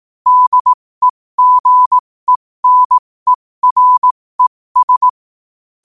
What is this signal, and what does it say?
Morse code, heard as a keyed tone, about 18 wpm, spaced as for 15 wpm, 1000 Hz. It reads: DEGENERES
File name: qotdfiles / 15wpm